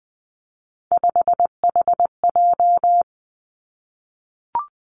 Morse code, heard as a keyed tone, about 19 wpm, 700 Hz. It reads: 5HJ
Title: Three Challenging Letter-Number Combinations 20wpm